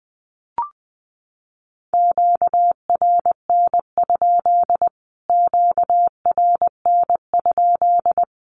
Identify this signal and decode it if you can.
Morse code, heard as a keyed tone, about 20 wpm, 700 Hz. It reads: QRN? QRN?
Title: Q-Codes Repeated 3x 20wpm